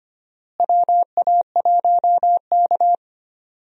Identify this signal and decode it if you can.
Morse code, heard as a keyed tone, about 25 wpm, 700 Hz. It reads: WA1K
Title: USA Call Signs - 4 Characters - Repeated 3x - 25wpm